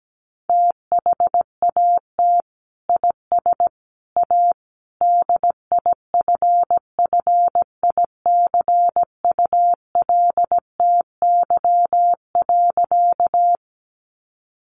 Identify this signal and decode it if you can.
Morse code, heard as a keyed tone, about 17 wpm, 700 Hz. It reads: THAT IS A DIFFICULTY.
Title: Sentences from Top 1500 Words 17wpm